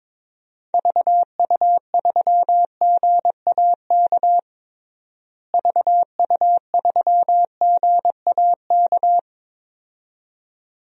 Morse code, heard as a keyed tone, about 22 wpm, 700 Hz. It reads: VU3GAK VU3GAK
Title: DXCC-100 Call Signs - 6 Characters - Repeated 2x - 22wpm